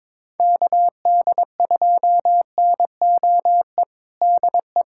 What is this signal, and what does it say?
Morse code, heard as a keyed tone, about 23 wpm, 700 Hz. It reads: KD2NOE DE
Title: USA Callsign DE Callsign - 6 Characters - 22wpm